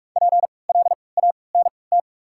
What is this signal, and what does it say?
Morse code, heard as a keyed tone, about 47 wpm, 690 Hz. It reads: P L A N T
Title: 3x Character Spacing - Top 300 Words - 45wpm